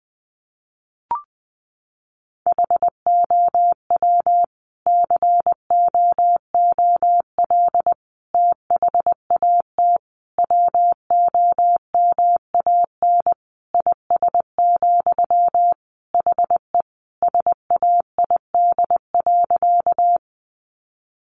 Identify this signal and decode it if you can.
Morse code, heard as a keyed tone, about 20 wpm, 700 Hz. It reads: HOW COOL THAT WOMAN IS, HE SAID.
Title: Sentences from Top 1200 Words 20wpm